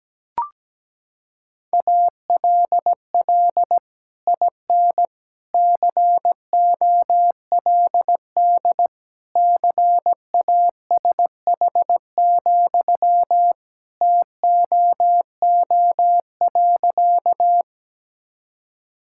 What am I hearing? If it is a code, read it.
Morse code, heard as a keyed tone, about 17 wpm, 700 Hz. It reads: ALL IN COLD CASH, TOO.
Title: Sentences from Top 1000 Words - Review Part 5 - 17wpm